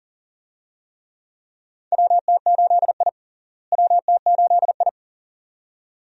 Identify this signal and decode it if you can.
Morse code, heard as a keyed tone, about 40 wpm, 700 Hz. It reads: WT8I WT8I
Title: USA Call Signs - 4 Characters - Repeated 2x - 40wpm